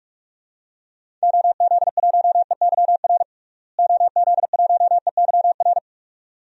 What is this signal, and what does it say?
Morse code, heard as a keyed tone, about 44 wpm, 700 Hz. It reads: OZ1EYR OZ1EYR
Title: DXCC-100 Call Signs - 6 Characters - Repeated 2x - 45wpm